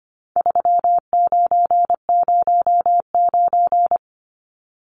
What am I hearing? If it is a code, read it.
Morse code, heard as a keyed tone, about 25 wpm, 700 Hz. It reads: 3909
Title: Numbers - 4-Digit 25wpm